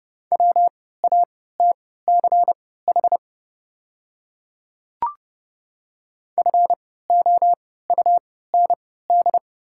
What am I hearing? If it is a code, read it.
Morse code, heard as a keyed tone, about 30 wpm, spaced as for 17 wpm, 690 Hz. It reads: WATCH FOUND
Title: 3x Character Spacing - Top 500 Words - 30wpm